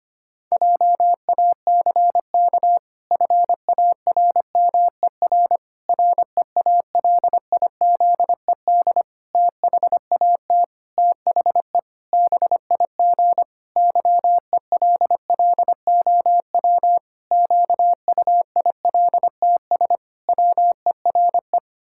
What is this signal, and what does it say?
Morse code, heard as a keyed tone, about 25 wpm, 700 Hz. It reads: JACK FARMER REALIZED THAT THE BIG YELLOW QUILTS WERE
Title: Pangram Challenge v2 - 25wpm